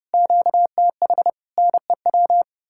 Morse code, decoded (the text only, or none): QTH NEW